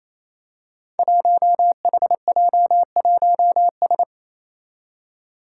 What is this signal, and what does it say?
Morse code, heard as a keyed tone, about 27 wpm, 700 Hz. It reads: 1HJ1S
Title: Five Challenging Letter-Number Combinations 28wpm